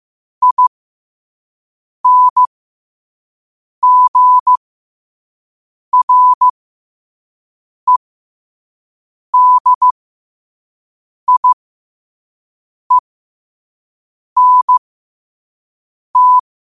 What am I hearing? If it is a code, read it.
Morse code, heard as a keyed tone, about 15 wpm, spaced as for 5 wpm, 1000 Hz. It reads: INGREDIENT
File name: 2024-10-07-5wpm